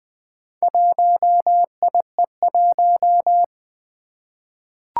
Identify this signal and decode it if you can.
Morse code, heard as a keyed tone, about 20 wpm, 700 Hz. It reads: 1IE1
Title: Four Challenging Letter-Number Combinations 20wpm